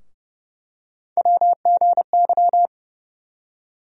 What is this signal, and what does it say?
Morse code, heard as a keyed tone, about 30 wpm, 700 Hz. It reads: WGY
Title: USA Call Sign Postfixes - 3 Letters 30wpm